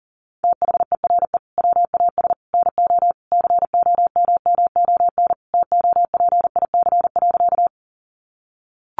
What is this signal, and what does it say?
Morse code, heard as a keyed tone, about 40 wpm, 700 Hz. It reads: THERE WAS NO COMMON TOPIC.
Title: Sentences from Top 1200 Words 40wpm